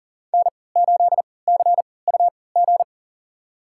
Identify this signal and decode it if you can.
Morse code, heard as a keyed tone, about 40 wpm, spaced as for 23 wpm, 700 Hz. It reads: N8CUG
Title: USA Call Signs - 5 Characters - 3x Character Spacing - 40wpm